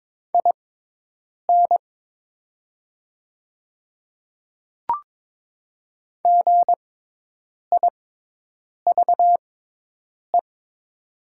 Morse code, heard as a keyed tone, about 22 wpm, spaced as for 8 wpm, 700 Hz. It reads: IN GIVE
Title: 6x Character Spacing - Top 200 Words - 22wpm